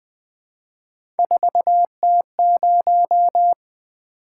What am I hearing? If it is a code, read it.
Morse code, heard as a keyed tone, about 20 wpm, 700 Hz. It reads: 4T0
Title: Three Letter-Number Combinations - 20wpm